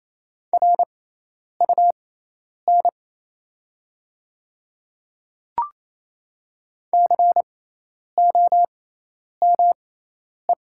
Morse code, heard as a keyed tone, about 28 wpm, spaced as for 10 wpm, 700 Hz. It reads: RUN COME
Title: 6x Character Spacing - Top 300 Words - 28wpm